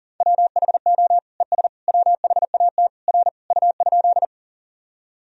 Morse code, heard as a keyed tone, about 40 wpm, 700 Hz. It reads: WHO ES WHAT R U?
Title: Sentences from Top 100 Words and Abbreviations 40wpm